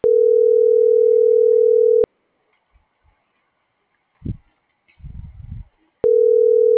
Ringback tone (North America), 440 + 480 Hz.